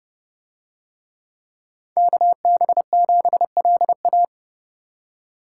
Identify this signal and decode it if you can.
Morse code, heard as a keyed tone, about 28 wpm, 700 Hz. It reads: KB7LA